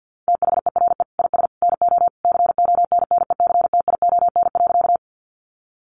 Morse code, heard as a keyed tone, about 50 wpm, 700 Hz. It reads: THERE IS NO CONNECTION.
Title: Sentences from Top 1300 Words 50wpm